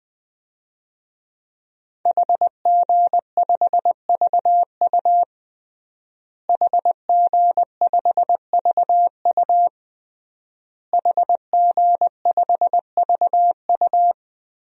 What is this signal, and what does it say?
Morse code, heard as a keyed tone, about 20 wpm, 700 Hz. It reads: HG5VU HG5VU HG5VU
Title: DXCC-100 Call Signs - 5 Characters - Repeated 3x - 20wpm